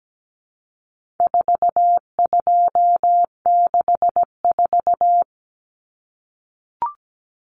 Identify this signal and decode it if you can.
Morse code, heard as a keyed tone, about 17 wpm, 700 Hz. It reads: 4264
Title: Numbers - 4-Digit 17wpm